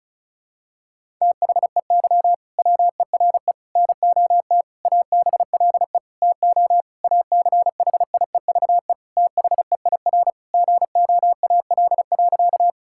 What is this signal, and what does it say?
Morse code, heard as a keyed tone, about 36 wpm, 700 Hz. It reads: THEY WERE NOT ABLE TO ACHIEVE THEIR GOAL.
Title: Sentences from Top 1000 Words - Review Part 6 - 35wpm